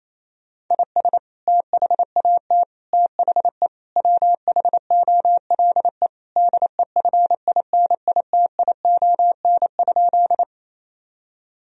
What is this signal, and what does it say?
Morse code, heard as a keyed tone, about 28 wpm, 700 Hz. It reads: IS THAT THE WHOLE DEFINITION?
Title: Sentences from Top 1000 Words - Review Part 4 - 28wpm